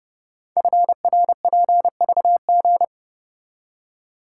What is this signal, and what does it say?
Morse code, heard as a keyed tone, about 30 wpm, 700 Hz. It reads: FRPVG
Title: Code Groups - 5 Letters - 30wpm